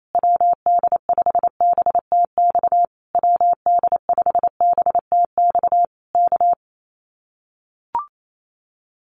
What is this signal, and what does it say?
Morse code, heard as a keyed tone, about 28 wpm, 700 Hz. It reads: WD5BTX WD5BTX K
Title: Common QSO Phrases Encore 28wpm